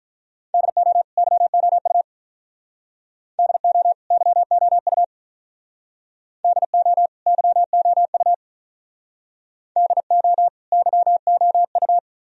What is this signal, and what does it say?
Morse code, heard as a keyed tone, about 45 wpm, 700 Hz. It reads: DO YOU DO YOU DO YOU DO YOU